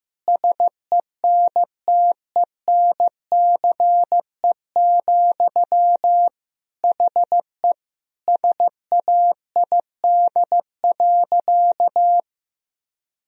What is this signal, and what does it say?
Morse code, heard as a keyed tone, about 15 wpm, 700 Hz. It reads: SENTENCE, HE SAID.
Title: Sentences from Top 1000 Words - Review Part 1 - 15wpm